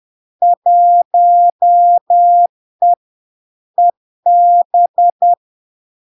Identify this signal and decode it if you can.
Morse code, heard as a keyed tone, about 10 wpm, 700 Hz. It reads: 1E EB
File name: LARK-FD-2022-Log-in-CW-10wpm-700-Hz